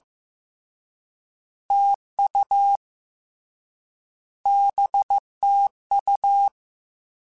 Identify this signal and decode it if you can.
Morse code, heard as a keyed tone, about 15 wpm, 800 Hz. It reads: TU BTU
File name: Misc / Misc-mp3s / Words @13